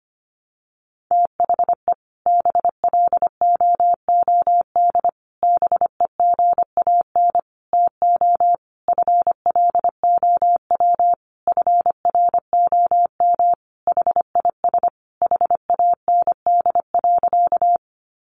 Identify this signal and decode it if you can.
Morse code, heard as a keyed tone, about 25 wpm, 700 Hz. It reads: THE BLOOD BEGAN TO FLOW FROM HIS HAND.
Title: Sentences from Top 1200 Words 25wpm